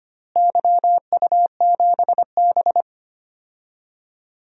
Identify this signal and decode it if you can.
Morse code, heard as a keyed tone, about 25 wpm, 700 Hz. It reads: YU7B
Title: DXCC-100 Call Signs - 4 Characters - Repeated 3x - 25wpm